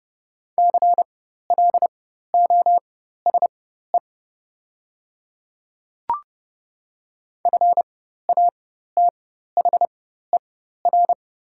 Morse code, heard as a keyed tone, about 30 wpm, spaced as for 14 wpm, 690 Hz. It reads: CLOSE FATHER
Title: 4x Character Spacing - Top 300 Words - 30wpm